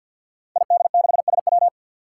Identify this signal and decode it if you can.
Morse code, heard as a keyed tone, about 47 wpm, 700 Hz. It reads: ID6SW
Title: Five Challenging Letter-Number Combinations 50wpm